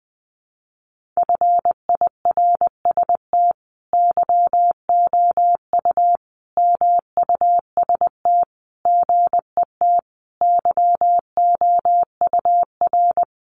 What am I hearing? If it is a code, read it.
Morse code, heard as a keyed tone, about 19 wpm, 700 Hz. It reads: FIRST YOU MUST GET YOUR
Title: Sentences from Top 1500 Words 20wpm